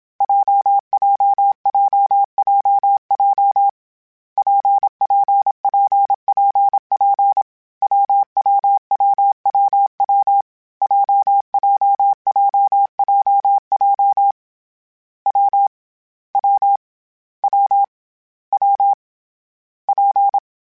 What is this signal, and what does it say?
Morse code, heard as a keyed tone, about 27 wpm, 800 Hz. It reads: JJJJJ PPPPP WWWWW JJJJJ W W W W P
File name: Difficult Combos / Difficult Combos-mp3s / WPJ WPJ1 @25